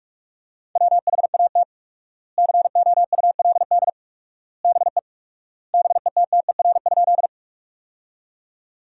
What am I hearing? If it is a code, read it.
Morse code, heard as a keyed tone, about 46 wpm, 700 Hz. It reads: WHAT COULD BE BETTER?
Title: 4x Word-Spacing - Sentences from Top 400 Words - 45wpm